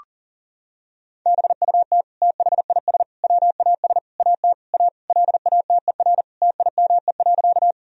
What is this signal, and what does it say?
Morse code, heard as a keyed tone, about 40 wpm, 700 Hz. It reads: BUT THIS WAS AT A LATER TIME.
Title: Sentences from Top 1000 Words - Review Part 4 - 40wpm